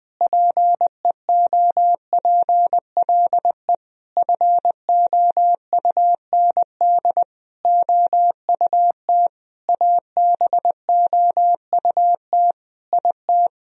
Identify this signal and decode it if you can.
Morse code, heard as a keyed tone, about 20 wpm, 700 Hz. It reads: PEOPLE FOUND OUT ABOUT IT
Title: Sentences from Top 1000 Words - Review Part 3 - 20wpm